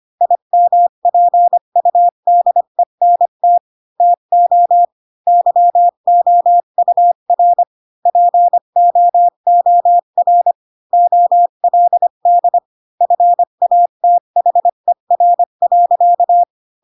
Morse code, heard as a keyed tone, about 25 wpm, 700 Hz. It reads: IMPUDENT TO YOUR POOR OLD FATHER.
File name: Pinocchio / Pinocchio ch3 20 25 ik1hge